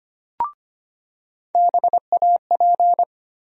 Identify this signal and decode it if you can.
Morse code, heard as a keyed tone, about 25 wpm, 700 Hz. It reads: BAP
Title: USA Call Sign Postfixes - 3 Letters 25wpm